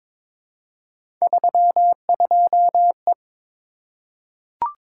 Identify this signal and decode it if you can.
Morse code, heard as a keyed tone, about 22 wpm, 700 Hz. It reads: 32E
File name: practice / 22wpm / Three Letter-Number Combinations - 22wpm